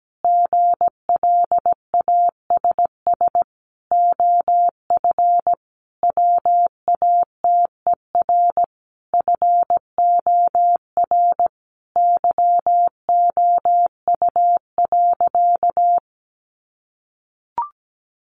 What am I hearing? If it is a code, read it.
Morse code, heard as a keyed tone, about 17 wpm, 700 Hz. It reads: GLASS OF WATER FOR YOU.
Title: Sentences from Top 1300 Words 17wpm